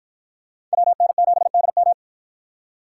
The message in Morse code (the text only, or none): WN7DM